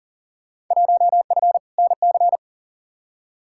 Morse code, heard as a keyed tone, about 40 wpm, 700 Hz. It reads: 1F NC
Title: Field Day Exchanges - 40wpm